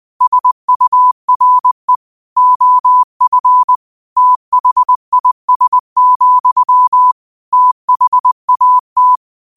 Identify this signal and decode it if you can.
Morse code, heard as a keyed tone, about 20 wpm, 1000 Hz. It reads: SURE OF THIS, THAT
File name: qotdfiles / 20wpm